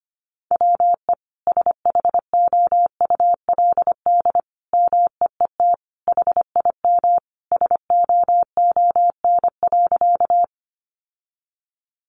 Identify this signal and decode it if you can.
Morse code, heard as a keyed tone, about 25 wpm, 700 Hz. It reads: WE SHOULD MEET HIM SOON.
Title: Sentences from Top 1100 Words 25wpm